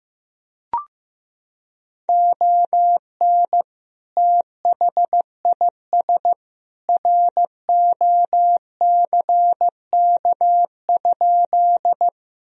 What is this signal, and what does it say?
Morse code, heard as a keyed tone, about 15 wpm, 700 Hz. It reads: ON THIS ROCK?
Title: Sentences from Top 1000 Words - Review Part 2 - 15wpm